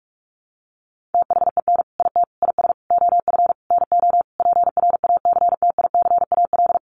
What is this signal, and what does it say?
Morse code, heard as a keyed tone, about 44 wpm, 700 Hz. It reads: THEN IT IS OF NO PRACTICAL
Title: Sentences from Top 1400 Words 45wpm